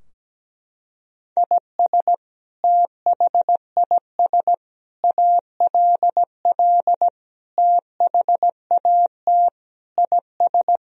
Morse code, heard as a keyed tone, about 17 wpm, 700 Hz. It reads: IS THIS ALL THAT IS
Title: Sentences from Top 1000 Words - Review Part 3 - 17wpm